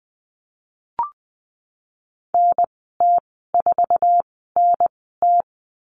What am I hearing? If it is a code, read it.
Morse code, heard as a keyed tone, about 20 wpm, spaced as for 15 wpm, 700 Hz. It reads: NT4NT
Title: 055 Lesson 10g - Letter r - Call Signs 20wpm